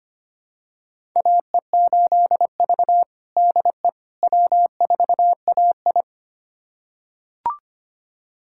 Morse code, heard as a keyed tone, about 25 wpm, 700 Hz. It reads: AE8V DE W4AI
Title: USA Callsign DE Callsign - 4 Characters - 25wpm